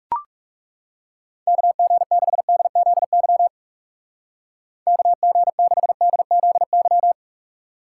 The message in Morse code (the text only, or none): KG6DZY KG6DZY